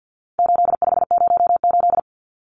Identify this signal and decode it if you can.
Morse code, heard as a keyed tone, about 50 wpm, 700 Hz. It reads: 8508
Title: Numbers - 4-Digit 50wpm